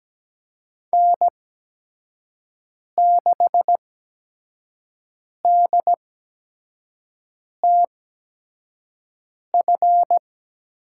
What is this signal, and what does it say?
Morse code, heard as a keyed tone, about 17 wpm, spaced as for 5 wpm, 700 Hz. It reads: N6DTF
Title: USA Call Signs - 8x Character Spacing - 17wpm